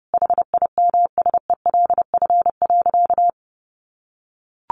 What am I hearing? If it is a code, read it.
Morse code, heard as a keyed tone, about 30 wpm, 700 Hz. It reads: HIMSELF.
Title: Sentences from Top 1500 Words 30wpm